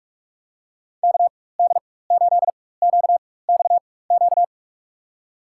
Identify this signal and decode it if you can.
Morse code, heard as a keyed tone, about 47 wpm, spaced as for 21 wpm, 700 Hz. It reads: KD8QXQ
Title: USA Call Signs - 4x Character Spacing - 45wpm